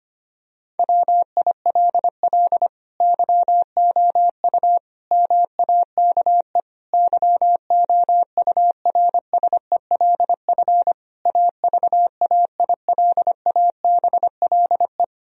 Morse code, heard as a keyed tone, about 25 wpm, 700 Hz. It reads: WILL YOU MAKE YOURSELF AVAILABLE
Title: Sentences from Top 1000 Words - Review Part 3 - 25wpm